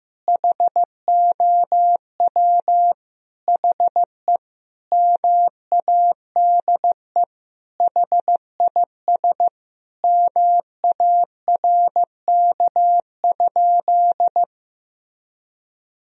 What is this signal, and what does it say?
Morse code, heard as a keyed tone, about 15 wpm, 700 Hz. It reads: HOW HE MADE HIS MARK?
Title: Sentences from Top 1000 Words - Review Part 1 - 15wpm